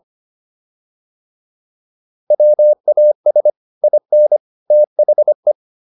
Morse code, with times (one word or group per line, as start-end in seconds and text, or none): 2.3-3.5 WAS
3.8-4.4 IN
4.7-5.5 THE